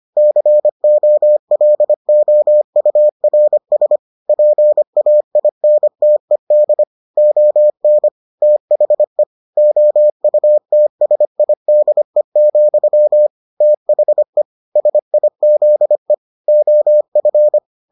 Morse code, heard as a keyed tone, about 25 wpm, 600 Hz. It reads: COLOURS PAINTED ON THE OUTSIDE, THE SIZE OF